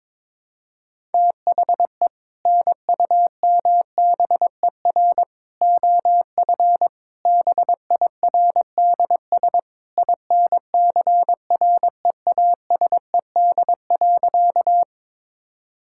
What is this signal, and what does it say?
Morse code, heard as a keyed tone, about 21 wpm, 700 Hz. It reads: THE NUMBER OF BIRDS INCREASED.
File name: practice / 22wpm / Sentences with 3-Word Phrases - 22wpm